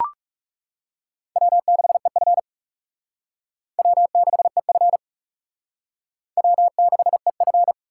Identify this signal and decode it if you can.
Morse code, heard as a keyed tone, about 36 wpm, 700 Hz. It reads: W6EF W6EF W6EF
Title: Speed Racing - USA 1x2 Call Signs 30wpm